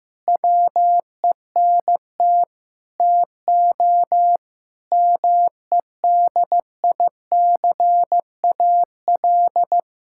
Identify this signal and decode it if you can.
Morse code, heard as a keyed tone, about 15 wpm, 700 Hz. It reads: WENT TO MEDICAL
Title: Sentences from Top 1000 Words - Review Part 5 - 15wpm